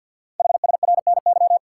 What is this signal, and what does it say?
Morse code, heard as a keyed tone, about 48 wpm, 690 Hz. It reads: HSRNY
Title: Code Groups - 5 Letters - 50wpm